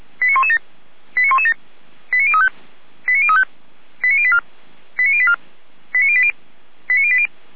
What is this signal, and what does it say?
Five-tone selective call (ZVEI), heard as quick runs of five tones, tones of about 70 ms. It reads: 89117 89117 89035 89035 89084 89084 89080 89080